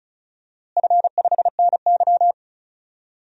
Morse code, heard as a keyed tone, about 34 wpm, 700 Hz. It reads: F5NY
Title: DXCC-100 Call Signs - 4 Characters - Repeated 2x - 35wpm